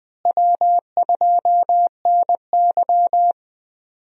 Morse code, heard as a keyed tone, about 20 wpm, 700 Hz. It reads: W2NY
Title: USA Call Signs - 4 Characters - Repeated 3x - 20wpm